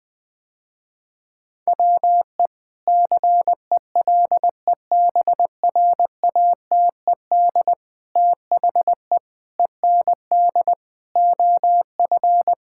Morse code, heard as a keyed tone, about 21 wpm, 700 Hz. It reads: WE CELEBRATED THE END OF
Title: Sentences with 5-Word Phrases - Spoken Beginning - 20wpm